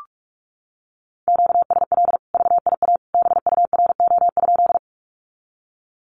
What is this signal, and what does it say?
Morse code, heard as a keyed tone, about 44 wpm, 700 Hz. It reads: QSL VIA BURO?
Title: Common QSO Phrases 45wpm